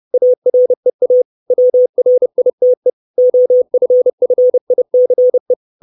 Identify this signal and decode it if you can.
Morse code, heard as a keyed tone, about 30 wpm, 500 Hz. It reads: AREA WRITE OFFICE
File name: 632 Words 30WPM